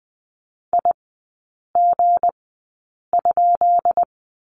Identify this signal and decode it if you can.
Morse code, heard as a keyed tone, about 20 wpm, spaced as for 8 wpm, 700 Hz. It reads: IG?
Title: 178 Lesson 29e - Slash - Three Characters 20wpm